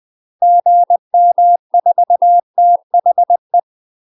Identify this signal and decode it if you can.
Morse code, heard as a keyed tone, about 20 wpm, 700 Hz. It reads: GM4THE
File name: UK MASTER QSOs TRAINING / 20WPM / UK MASTER QSOs 0039